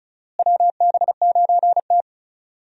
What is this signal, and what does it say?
Morse code, heard as a keyed tone, about 36 wpm, 700 Hz. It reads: WB9T
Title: USA Call Signs - 4 Characters - Repeated 2x - 35wpm